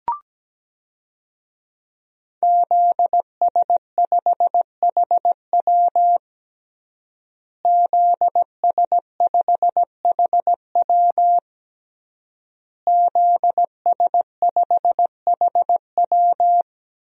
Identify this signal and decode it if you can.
Morse code, heard as a keyed tone, about 17 wpm, 700 Hz. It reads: ZS5HW ZS5HW ZS5HW
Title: DXCC-100 Call Signs - 5 Characters - Repeated 3x - 17wpm